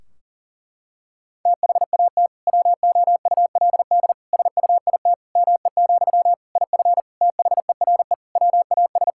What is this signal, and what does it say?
Morse code, heard as a keyed tone, about 37 wpm, 700 Hz. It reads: THAT WOULD SUIT ME, IF THERE WAS
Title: Sentences from Top 1600 Words 40wpm